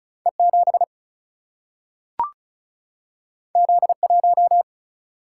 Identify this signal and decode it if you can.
Morse code, heard as a keyed tone, about 36 wpm, 700 Hz. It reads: E 7 Z 1
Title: Single-Letter Single-Number 35wpm